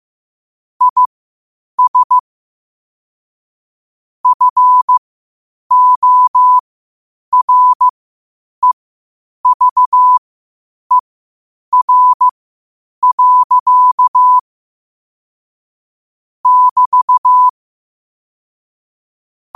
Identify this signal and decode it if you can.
Morse code, heard as a keyed tone, about 15 wpm, spaced as for 8 wpm, 1000 Hz. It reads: IS FOREVER. <BT>
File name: qotdfiles / 8wpm